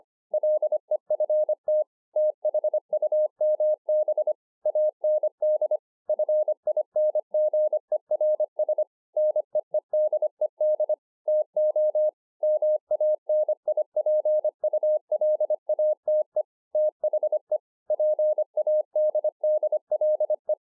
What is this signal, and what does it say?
Morse code, heard as a keyed tone, about 25 wpm, 610 Hz. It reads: LEFT THUMB AND FINGERS NEEDED TO MANIPULATE THE PADDLE